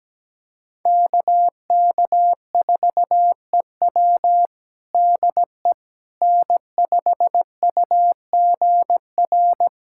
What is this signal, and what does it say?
Morse code, heard as a keyed tone, about 17 wpm, 700 Hz. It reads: KK4EW DE N5UGR
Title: USA Callsign DE Callsign - 5 Characters - 17wpm